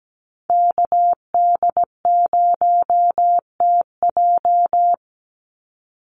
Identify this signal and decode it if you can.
Morse code, heard as a keyed tone, about 17 wpm, 700 Hz. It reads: KD0TJ
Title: USA Call Signs 17wpm